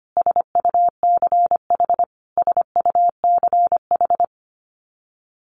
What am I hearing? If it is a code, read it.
Morse code, heard as a keyed tone, about 25 wpm, 700 Hz. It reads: SUCH SUCH
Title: Top 200 Words Repeated 25wpm